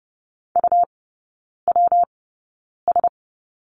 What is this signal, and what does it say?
Morse code, heard as a keyed tone, about 30 wpm, spaced as for 9 wpm, 700 Hz. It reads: UWS
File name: Lesson 19e - Letter w - Three Characters 30wpm